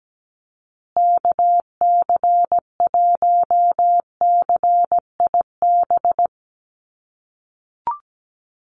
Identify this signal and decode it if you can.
Morse code, heard as a keyed tone, about 17 wpm, 700 Hz. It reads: KC1CIB
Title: USA Call Signs Repeated 3x 17wpm